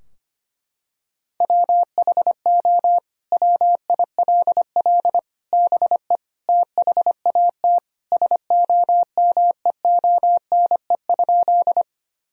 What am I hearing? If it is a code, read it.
Morse code, heard as a keyed tone, about 25 wpm, 700 Hz. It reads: WHO WILL BE THAT SOMEONE?
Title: Sentences from Top 1000 Words - Review Part 4 - 25wpm